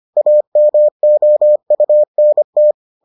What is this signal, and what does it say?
Morse code, heard as a keyed tone, about 25 wpm, 600 Hz. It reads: AMOUNT